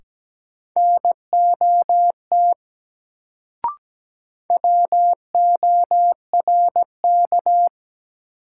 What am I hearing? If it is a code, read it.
Morse code, heard as a keyed tone, about 17 wpm, 700 Hz. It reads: NOT WORK
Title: Sets of 2 Words - New Word Drill - 17wpm